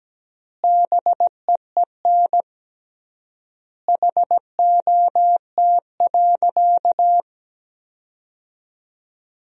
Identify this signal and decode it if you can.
Morse code, heard as a keyed tone, about 17 wpm, 700 Hz. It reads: BEEN HOT.
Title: 3x Word-Spacing - Sentences from Top 500 Words - 17wpm